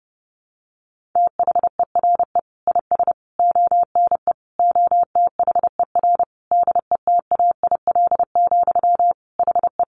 Morse code, heard as a keyed tone, about 30 wpm, 700 Hz. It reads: THERE IS ONE OTHER DETAIL, HE
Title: Sentences from Top 1500 Words 30wpm